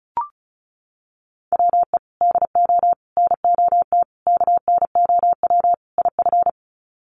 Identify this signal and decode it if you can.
Morse code, heard as a keyed tone, about 34 wpm, 700 Hz. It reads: WE DO NOT KNOW IF
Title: Sets of 5 Words Encore2 35wpm